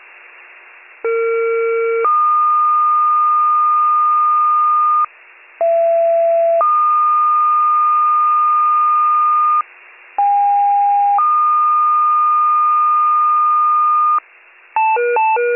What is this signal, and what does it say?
Two-tone page, heard as a one-second tone, then a three-second tone, about 460 then 1220 Hz, about 670 then 1220 Hz, about 810 then 1220 Hz. It is heard from 1.0 to 14.2 s.